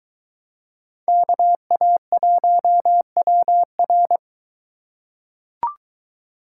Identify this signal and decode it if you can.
Morse code, heard as a keyed tone, about 23 wpm, 700 Hz. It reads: KA1WR